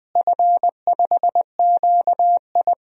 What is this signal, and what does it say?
Morse code, heard as a keyed tone, about 20 wpm, 700 Hz. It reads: F5QI
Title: DXCC-100 Call Signs - 4 Characters - Repeated 3x - 20wpm